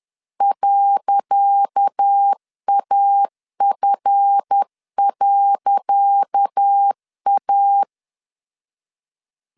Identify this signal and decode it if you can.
Morse code, heard as a keyed tone, about 10 wpm, 800 Hz. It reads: .AF.A
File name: LEZIONI / 10 SIGNS / 10C DOT 10wpm 2940